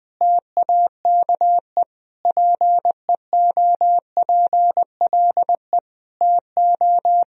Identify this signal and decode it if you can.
Morse code, heard as a keyed tone, about 20 wpm, 700 Hz. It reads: TAKE PEOPLE TO
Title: Sentences from Top 1000 Words - Review Part 4 - 20wpm